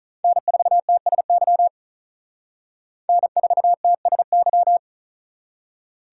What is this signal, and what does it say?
Morse code, heard as a keyed tone, about 39 wpm, 700 Hz. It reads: N4TSY N4TSY
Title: Speed Racing - USA 1x3 Call Signs 35wpm